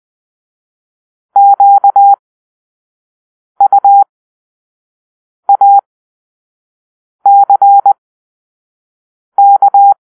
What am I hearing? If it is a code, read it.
Morse code, heard as a keyed tone, about 20 wpm, spaced as for 5 wpm, 800 Hz. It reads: QUACK
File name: Grp-5 Drill-2